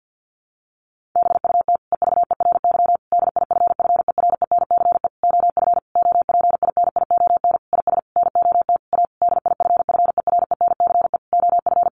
Morse code, heard as a keyed tone, about 50 wpm, 700 Hz. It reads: BUT EVERY DIFFERENCE OF OPINION IS NOT A DIFFERENCE OF
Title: Sentences from Top 1400 Words 50wpm